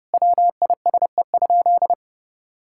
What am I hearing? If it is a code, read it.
Morse code, heard as a keyed tone, about 30 wpm, 700 Hz. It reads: WISE?
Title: Sentences from Top 1600 Words 30wpm